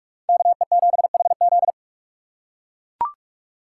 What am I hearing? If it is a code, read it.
Morse code, heard as a keyed tone, about 41 wpm, 700 Hz. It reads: KE7HZ